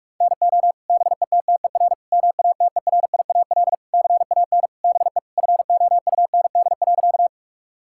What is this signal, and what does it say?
Morse code, heard as a keyed tone, about 42 wpm, 700 Hz. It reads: NO BETTER MATERIAL CAN BE FOUND.
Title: Sentences from Top 1000 Words - Review Part 5 - 45wpm